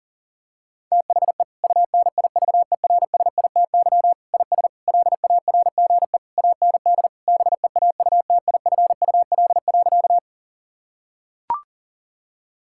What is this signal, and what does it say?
Morse code, heard as a keyed tone, about 40 wpm, 690 Hz. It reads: THE UNIVERSITY IS LARGE AND BEAUTIFUL.
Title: Sentences from Top 1000 Words - Review Part 5 - 40wpm